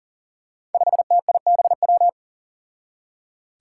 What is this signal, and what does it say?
Morse code, heard as a keyed tone, about 39 wpm, 700 Hz. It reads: 5TIBW
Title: Five Challenging Letter-Number Combinations 40wpm